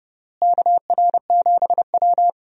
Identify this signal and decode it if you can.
Morse code, heard as a keyed tone, about 30 wpm, 700 Hz. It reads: KR7W
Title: USA Call Signs - 4 Characters - Repeated 3x - 30wpm